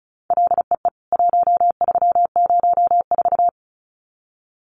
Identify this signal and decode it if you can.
Morse code, heard as a keyed tone, about 34 wpm, 700 Hz. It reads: LEE 1304
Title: CWT Exchange 35wpm